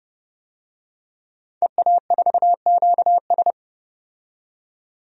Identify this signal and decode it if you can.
Morse code, heard as a keyed tone, about 30 wpm, 700 Hz. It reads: EA4QS
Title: DXCC-100 Call Signs - 5 Characters - 30wpm